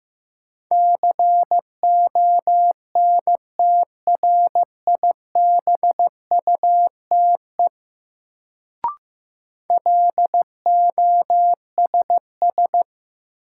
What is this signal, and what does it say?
Morse code, heard as a keyed tone, about 15 wpm, 700 Hz. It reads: CONTRIBUTE LOSS
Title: Sentences from Top 1400 Words - New Word Drill - 15wpm